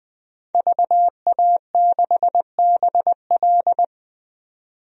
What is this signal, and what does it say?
Morse code, heard as a keyed tone, about 21 wpm, 700 Hz. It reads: VA6BL
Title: DXCC-100 Call Signs - 5 Characters - 20wpm